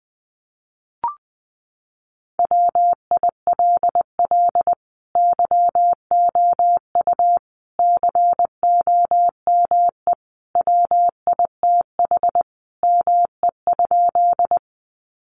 Morse code, heard as a keyed tone, about 20 wpm, 700 Hz. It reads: WILL YOU COME WITH ME?
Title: Sentences from Top 200 Words 20wpm